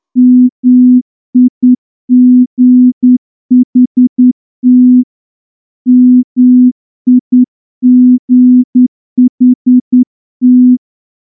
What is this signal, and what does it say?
Morse code, heard as a keyed tone, about 10 wpm, 250 Hz. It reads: MIGHT MIGHT
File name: morse-training-10WPM